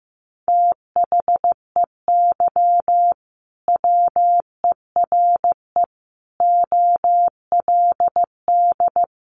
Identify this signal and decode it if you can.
Morse code, heard as a keyed tone, about 15 wpm, 700 Hz. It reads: THEY WERE OLD
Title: Sentences from Top 800 Words 15wpm